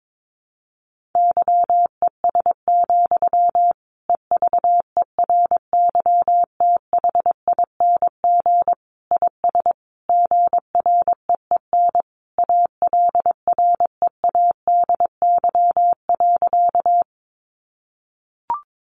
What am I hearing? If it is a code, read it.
Morse code, heard as a keyed tone, about 22 wpm, 700 Hz. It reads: YES, EVERYTHING IS GREEN ALREADY.
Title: Sentences from Top 700 Words 22wpm